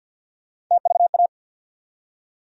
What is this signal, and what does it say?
Morse code, heard as a keyed tone, about 53 wpm, 700 Hz. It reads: TVA
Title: USA Call Sign Postfixes - 3 Letters 50wpm